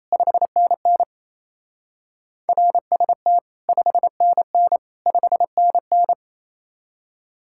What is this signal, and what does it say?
Morse code, heard as a keyed tone, about 34 wpm, 700 Hz. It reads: 5NN RST 5NN 5NN
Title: Speed-Racing Common QSO Phrases Encore - 28wpm